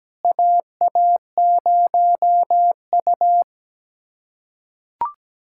Morse code, heard as a keyed tone, about 17 wpm, 700 Hz. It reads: AA0U
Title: USA Callsign DE Callsign - 4 Characters - 17wpm